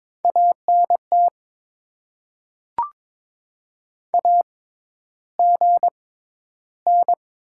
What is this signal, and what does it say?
Morse code, heard as a keyed tone, about 21 wpm, spaced as for 11 wpm, 700 Hz. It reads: ANT AGN
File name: Top 90 QSO Elements - 6x Character Spacing - 22wpm